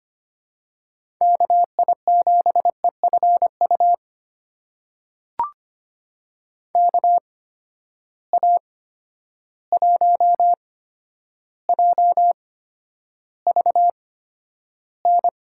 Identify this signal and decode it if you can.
Morse code, heard as a keyed tone, about 25 wpm, spaced as for 11 wpm, 700 Hz. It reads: KI7EFU KA1JVN
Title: USA Call Signs - 6 Characters - 8x Character Spacing - 25wpm